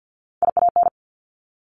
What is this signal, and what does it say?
Morse code, heard as a keyed tone, about 50 wpm, 700 Hz. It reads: IAN
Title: Top Three Letter Combinations in Words 50wpm